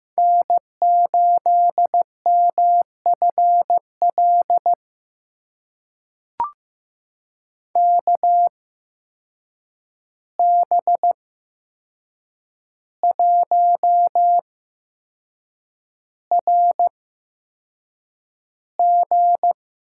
Morse code, heard as a keyed tone, about 15 wpm, spaced as for 6 wpm, 700 Hz. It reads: N8MFL KB1RG
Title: USA Call Signs - 8x Character Spacing - 15wpm